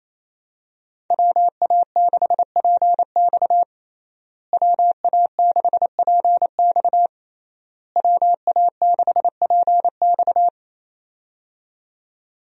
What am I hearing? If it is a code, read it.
Morse code, heard as a keyed tone, about 28 wpm, 700 Hz. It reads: WA6PX WA6PX WA6PX
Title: USA Call Signs - 5 Characters - Repeated 3x - 28wpm